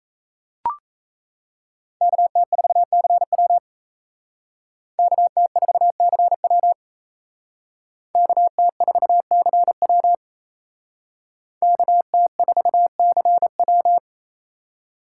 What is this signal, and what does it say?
Morse code, heard as a keyed tone, about 32 wpm, 700 Hz. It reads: KT4CW KT4CW KT4CW KT4CW
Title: Speed Racing - USA 2x2 Call Signs 28wpm